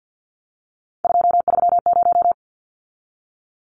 Morse code, heard as a keyed tone, about 50 wpm, 700 Hz. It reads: 230
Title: Numbers - 3-Digit 50wpm